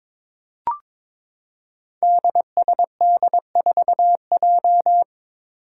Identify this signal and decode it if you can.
Morse code, heard as a keyed tone, about 22 wpm, 700 Hz. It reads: DSD4J
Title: Five Challenging Letter-Number Combinations 22wpm